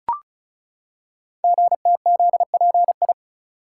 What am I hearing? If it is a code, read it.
Morse code, heard as a keyed tone, about 36 wpm, 700 Hz. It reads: GTZPI
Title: Code Groups - 5 Letters - 35wpm